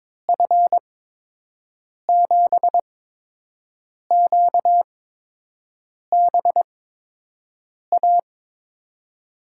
Morse code, heard as a keyed tone, about 21 wpm, spaced as for 6 wpm, 700 Hz. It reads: F7QBA
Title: USA Call Signs - 6 Characters - 8x Character Spacing - 22wpm